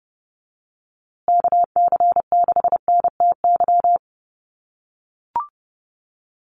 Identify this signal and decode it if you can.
Morse code, heard as a keyed tone, about 30 wpm, 700 Hz. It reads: KC6NTY
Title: USA Call Signs Repeated 2x 30wpm